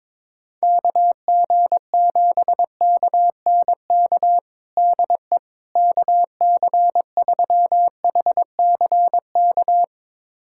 Morse code, heard as a keyed tone, about 22 wpm, 700 Hz. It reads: KG7KNK DE KC3HCK
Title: USA Callsign DE Callsign - 6 Characters - 22wpm